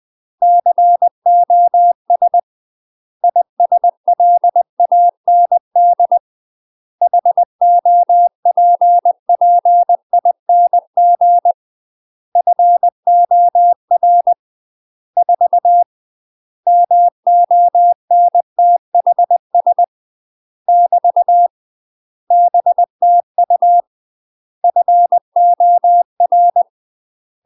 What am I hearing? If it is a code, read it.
Morse code, heard as a keyed tone, about 20 wpm, 700 Hz. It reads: COS ISLAND HOPPING FOR 4 MONTHS <BT> BTU FOR